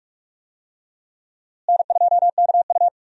Morse code, heard as a keyed tone, about 46 wpm, 700 Hz. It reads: N2KU
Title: USA Call Signs - 4 Characters - Repeated 2x - 45wpm